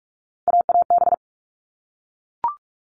Morse code, heard as a keyed tone, about 44 wpm, 700 Hz. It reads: AAB
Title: All Cut Numbers - 3-Digit 45wpm